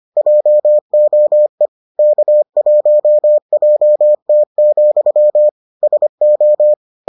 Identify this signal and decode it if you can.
Morse code, heard as a keyed tone, about 25 wpm, 600 Hz. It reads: JOE K1JT, SO